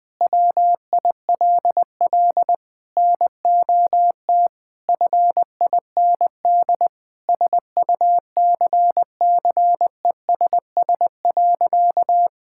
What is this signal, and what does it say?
Morse code, heard as a keyed tone, about 20 wpm, 700 Hz. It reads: WILL NOT FIND SUCCESS.
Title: Sentences from Top 1000 Words - Review Part 5 - 20wpm